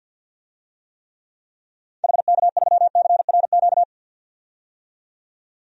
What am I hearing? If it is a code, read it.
Morse code, heard as a keyed tone, about 52 wpm, 700 Hz. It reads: HK3CRQ
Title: DXCC-100 Call Signs - 6 Characters - 50wpm